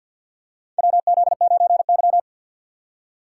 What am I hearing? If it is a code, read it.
Morse code, heard as a keyed tone, about 52 wpm, 700 Hz. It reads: WZ9Y
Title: USA Call Signs - 4 Characters - 6x Character Spacing - 50wpm